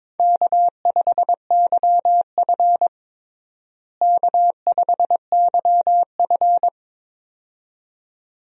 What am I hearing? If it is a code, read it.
Morse code, heard as a keyed tone, about 22 wpm, 700 Hz. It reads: K5YF K5YF
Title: USA Call Signs - 4 Characters - Repeated 2x - 22wpm